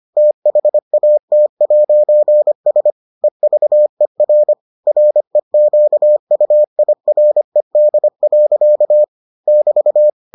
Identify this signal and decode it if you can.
Morse code, heard as a keyed tone, about 25 wpm, 600 Hz. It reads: THAT'S EVER REQUIRED. <BT>